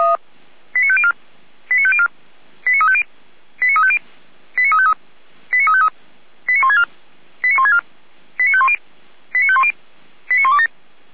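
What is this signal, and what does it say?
Five-tone selective call (ZVEI), heard as quick runs of five tones, tones of about 70 ms. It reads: 89504 89504 89350 89350 89353 89353 89164 89164 89510 89510 89127
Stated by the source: ZVEI five-tone call 89235 received on SDR, archived by feuerwehr-orlamuende.org fire brigade.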